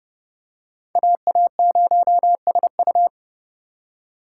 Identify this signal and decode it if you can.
Morse code, heard as a keyed tone, about 29 wpm, 700 Hz. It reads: AA0SU